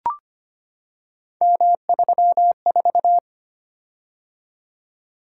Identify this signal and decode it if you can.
Morse code, heard as a keyed tone, about 25 wpm, 700 Hz. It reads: M34
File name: Three Letter-Number Combinations - 25wpm